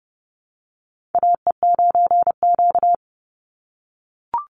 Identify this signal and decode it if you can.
Morse code, heard as a keyed tone, about 30 wpm, 700 Hz. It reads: AE9Q
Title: NA SOTA Chaser Callsigns 30wpm